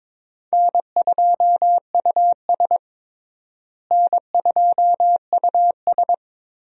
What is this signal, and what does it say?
Morse code, heard as a keyed tone, about 22 wpm, 700 Hz. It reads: N2US N2US
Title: USA Call Signs - 4 Characters - Repeated 3x - 22wpm